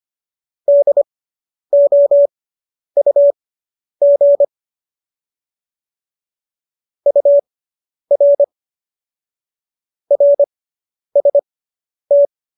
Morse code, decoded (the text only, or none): DOUG UR RST